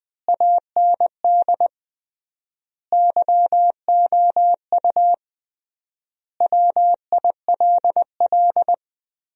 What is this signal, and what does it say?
Morse code, heard as a keyed tone, about 20 wpm, 700 Hz. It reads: AND YOU WILL
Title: 3x Word-Spacing - Sentences from Top 400 Words - 20wpm